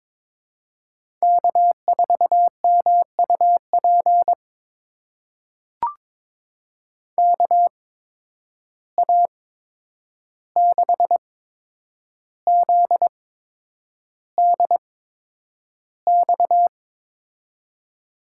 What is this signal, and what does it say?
Morse code, heard as a keyed tone, about 22 wpm, spaced as for 9 wpm, 700 Hz. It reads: K4MUP KA6ZDX
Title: USA Call Signs - 8x Character Spacing - 22wpm